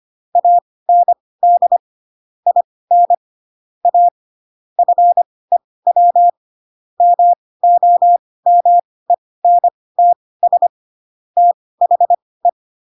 Morse code, heard as a keyed tone, about 25 wpm, spaced as for 18 wpm, 700 Hz. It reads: AND IN A FEW MOMENTS THE
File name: Pinocchio / Pinocchio ch5 12 25 ik1hge